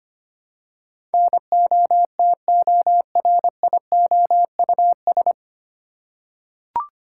Morse code, heard as a keyed tone, about 25 wpm, 700 Hz. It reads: NOTORIOUS
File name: Nine Letter Words - 25wpm